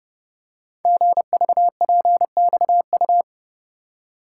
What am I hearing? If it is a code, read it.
Morse code, heard as a keyed tone, about 30 wpm, 700 Hz. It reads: GVPXU
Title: Code Groups - 5 Letters - 30wpm